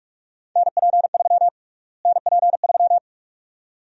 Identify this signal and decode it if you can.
Morse code, heard as a keyed tone, about 47 wpm, 700 Hz. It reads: NP3 NP3
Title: USA Call Sign Prefixes - Repeated 2x - 45wpm